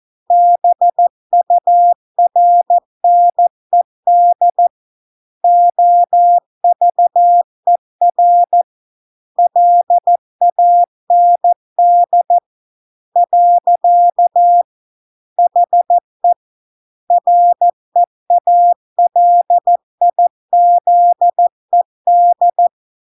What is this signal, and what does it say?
Morse code, heard as a keyed tone, about 14 wpm, 700 Hz. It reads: BURNED OVER LAND . HE REALIZED